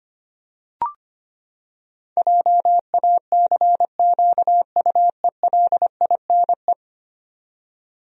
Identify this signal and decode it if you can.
Morse code, heard as a keyed tone, about 25 wpm, 700 Hz. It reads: JACQUELINE
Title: USA Common Names - 25wpm